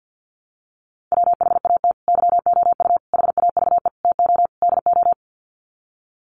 Morse code, heard as a keyed tone, about 50 wpm, 700 Hz. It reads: WHAT YOU HAVE TO DO
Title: Sets of 5 Words Encore 50wpm